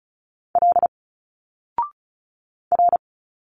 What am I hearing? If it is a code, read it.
Morse code, heard as a keyed tone, about 36 wpm, 700 Hz. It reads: L R
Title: Single Letters 35wpm